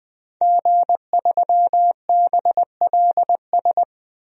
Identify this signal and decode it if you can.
Morse code, heard as a keyed tone, about 20 wpm, 700 Hz. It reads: G3BLS
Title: DXCC-100 Call Signs - 5 Characters - 20wpm